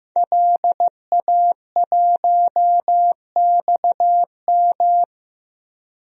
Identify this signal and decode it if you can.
Morse code, heard as a keyed tone, about 15 wpm, 700 Hz. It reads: LA1XM
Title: DXCC-100 Call Signs - 5 Characters - 15wpm